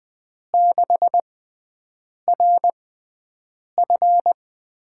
Morse code, heard as a keyed tone, about 20 wpm, spaced as for 7 wpm, 700 Hz. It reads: 6RF
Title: USA Call Signs - 5 Characters - 6x Character Spacing - 20wpm